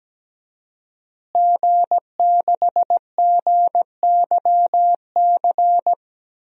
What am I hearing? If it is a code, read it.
Morse code, heard as a keyed tone, about 17 wpm, 700 Hz. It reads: G6GYC
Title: DXCC-100 Call Signs - 5 Characters - Repeated 2x - 17wpm